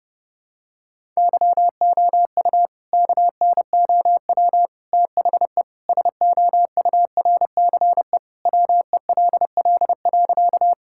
Morse code, heard as a keyed tone, about 30 wpm, 700 Hz. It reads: YOU KNOW THE SOURCE WELL.
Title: Sentences from Top 1000 Words - Review Part 5 - 30wpm